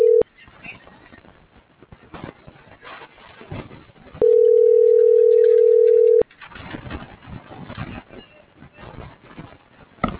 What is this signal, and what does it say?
Ringback tone (North America), 440 + 480 Hz.